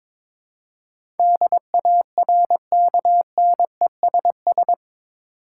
Morse code, heard as a keyed tone, about 22 wpm, 700 Hz. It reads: DARKNESS
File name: practice / 22wpm / Eight Letter Words - 22wpm